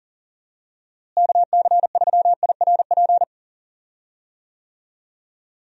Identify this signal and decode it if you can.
Morse code, heard as a keyed tone, about 40 wpm, 700 Hz. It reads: KC3IRP
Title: USA Call Signs - 6 Characters - Repeated 3x - 40wpm